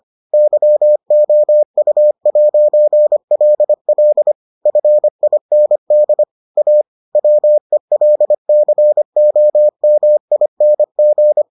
Morse code, heard as a keyed tone, about 25 wpm, 600 Hz. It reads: YOU'LL FIND A WELCOMING